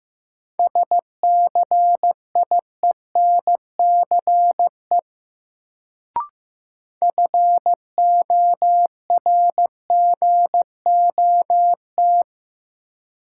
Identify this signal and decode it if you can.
Morse code, heard as a keyed tone, about 15 wpm, 700 Hz. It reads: SCIENCE FORGOT
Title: Sentences from Top 1400 Words - New Word Drill Alt - 15wpm